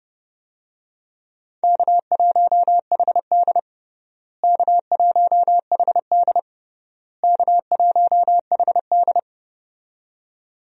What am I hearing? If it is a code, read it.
Morse code, heard as a keyed tone, about 30 wpm, 700 Hz. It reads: K1HD K1HD K1HD
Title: USA Call Signs - 4 Characters - Repeated 3x - 30wpm